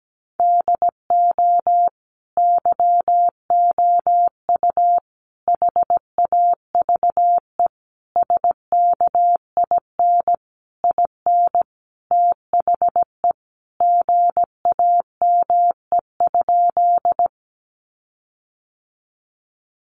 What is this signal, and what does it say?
Morse code, heard as a keyed tone, about 17 wpm, 700 Hz. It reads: DO YOU HAVE SKIN IN THE GAME?
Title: Sentences from Top 1300 Words 17wpm